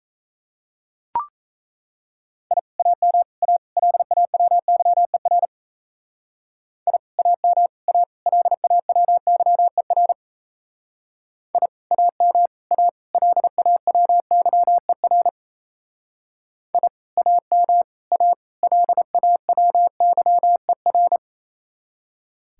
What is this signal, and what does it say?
Morse code, heard as a keyed tone, about 36 wpm, 700 Hz. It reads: I AM A LAWYER I AM A LAWYER I AM A LAWYER I AM A LAWYER